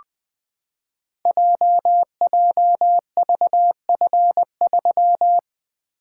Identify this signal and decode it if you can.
Morse code, heard as a keyed tone, about 20 wpm, 700 Hz. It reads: JJVF3
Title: Code Groups - 5 Letter:Numbers - 20wpm